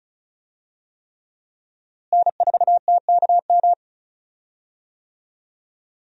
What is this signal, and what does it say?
Morse code, heard as a keyed tone, about 34 wpm, 700 Hz. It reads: N4TKM
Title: USA Call Signs - 5 Characters - Repeated 3x - 35wpm